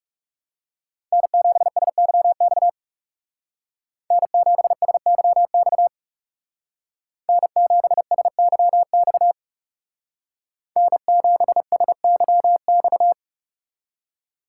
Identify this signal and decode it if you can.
Morse code, heard as a keyed tone, about 39 wpm, 700 Hz. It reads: N7SYX N7SYX N7SYX N7SYX